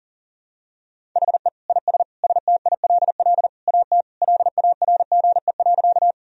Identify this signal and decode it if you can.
Morse code, heard as a keyed tone, about 37 wpm, 700 Hz. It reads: HE IS STILL AT LARGE.
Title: Sentences from Top 1000 Words - Review Part 1 - 40wpm